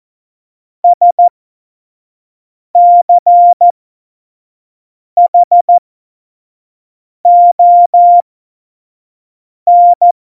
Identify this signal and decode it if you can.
Morse code, heard as a keyed tone, about 14 wpm, spaced as for 5 wpm, 700 Hz. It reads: SCHON
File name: Morsekurs / Witze / Witz 14. Januar 2020, 25 F70